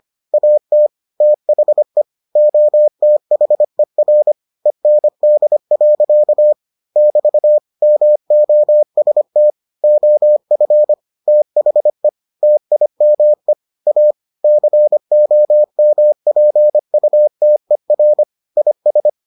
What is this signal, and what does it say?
Morse code, heard as a keyed tone, about 25 wpm, 600 Hz. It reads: AT THE OTHER END. <BT> MOST OF THE TIME A COMPUTER IS